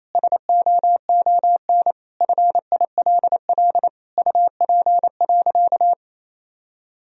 Morse code, heard as a keyed tone, about 28 wpm, 700 Hz. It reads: SOON FILL UP.
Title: Sentences from Top 1000 Words - Review Part 3 - 28wpm